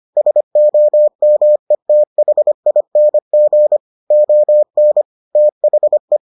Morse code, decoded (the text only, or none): SOMETHING ON THE